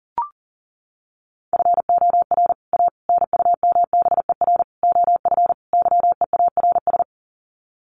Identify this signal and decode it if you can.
Morse code, heard as a keyed tone, about 40 wpm, 700 Hz. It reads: FOR A NUMBER OF YEARS
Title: Sets of 5 Words Encore 40wpm